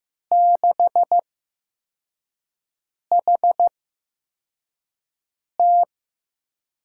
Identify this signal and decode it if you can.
Morse code, heard as a keyed tone, about 15 wpm, spaced as for 4 wpm, 700 Hz. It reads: 6HT
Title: USA Call Signs - 5 Characters - 8x Character Spacing - 15wpm